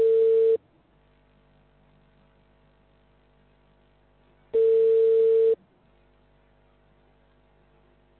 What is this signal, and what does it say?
Ringback tone (China, 450 Hz).